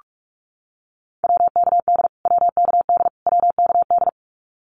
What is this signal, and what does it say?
Morse code, heard as a keyed tone, about 44 wpm, 700 Hz. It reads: WKD WKD WKD
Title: Abbreviations Repeated 3x - 45wpm